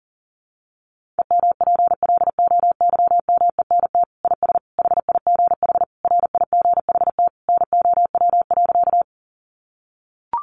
Morse code, heard as a keyed tone, about 40 wpm, 700 Hz. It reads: EMPLOYMENT IS HIGH RIGHT NOW.
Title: Sentences from Top 1500 Words 40wpm